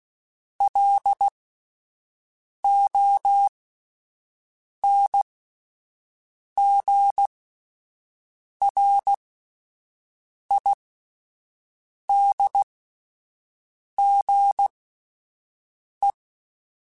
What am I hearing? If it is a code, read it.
Morse code, heard as a keyed tone, about 16 wpm, spaced as for 5 wpm, 800 Hz. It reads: LONGRIDGE